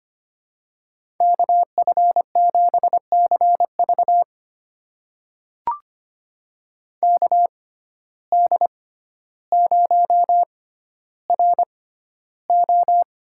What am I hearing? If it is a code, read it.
Morse code, heard as a keyed tone, about 25 wpm, spaced as for 13 wpm, 700 Hz. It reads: KF7CV KD0RO
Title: USA Call Signs - 5 Characters - 6x Character Spacing - 25wpm